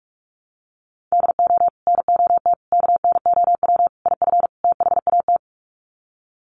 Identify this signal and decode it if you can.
Morse code, heard as a keyed tone, about 47 wpm, 700 Hz. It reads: DO NOT KNOW IF THAT
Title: Sets of 5 Words Encore2 45wpm